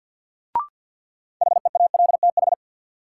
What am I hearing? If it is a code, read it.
Morse code, heard as a keyed tone, about 50 wpm, 700 Hz. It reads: HEALTH
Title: Sentences from Top 600 Words - New Word Drill - 50wpm